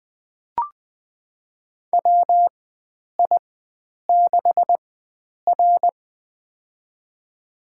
Morse code, heard as a keyed tone, about 20 wpm, spaced as for 9 wpm, 700 Hz. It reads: WI6R
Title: USA Call Signs - 4 Characters - 4x Character Spacing - 20wpm